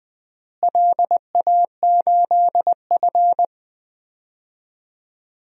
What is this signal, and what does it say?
Morse code, heard as a keyed tone, about 21 wpm, 700 Hz. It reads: LA8F
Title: DXCC-100 Call Signs - 4 Characters - Repeated 3x - 20wpm